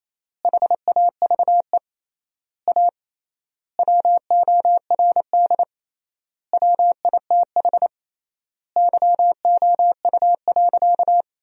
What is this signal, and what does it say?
Morse code, heard as a keyed tone, about 28 wpm, 700 Hz. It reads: HAVE A WORD WITH YOU.
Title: 3x Word-Spacing - Sentences from Top 300 Words - 28wpm